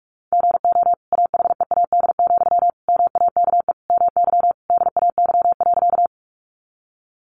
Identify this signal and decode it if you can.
Morse code, heard as a keyed tone, about 44 wpm, 700 Hz. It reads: GO AHEAD, MAKE MY DAY.
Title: Sentences from Top 1100 Words 45wpm